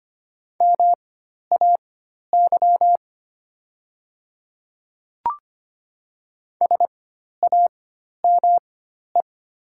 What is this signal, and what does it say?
Morse code, heard as a keyed tone, about 25 wpm, spaced as for 12 wpm, 700 Hz. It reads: MAY SAME
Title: 4x Character Spacing - Top 300 Words - 25wpm